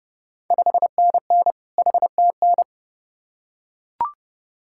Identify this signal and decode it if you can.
Morse code, heard as a keyed tone, about 31 wpm, 700 Hz. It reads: 5NN HTN
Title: Ontario QSO Party Exchange 30wpm